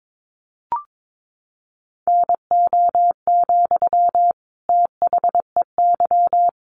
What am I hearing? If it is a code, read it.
Morse code, heard as a keyed tone, about 22 wpm, 700 Hz. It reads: NO, THEY
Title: Sentences from Top 1300 Words 22wpm